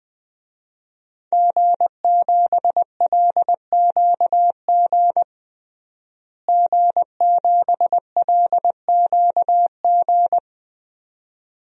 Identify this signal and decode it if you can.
Morse code, heard as a keyed tone, about 20 wpm, 700 Hz. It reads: G7LQG G7LQG
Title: DXCC-100 Call Signs - 5 Characters - Repeated 2x - 20wpm